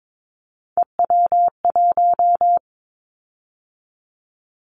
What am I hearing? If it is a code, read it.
Morse code, heard as a keyed tone, about 22 wpm, 700 Hz. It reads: EW1
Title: Callsign Prefixes for Top 50 DXCC - 22wpm